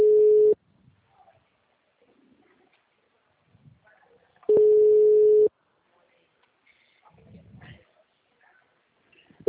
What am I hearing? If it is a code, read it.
Ringback tone (Europe, 425 Hz).
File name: out- -2011- - - .1255